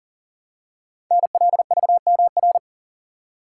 Morse code, heard as a keyed tone, about 41 wpm, 700 Hz. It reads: NLVMR
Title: Code Groups - 5 Letters - 40wpm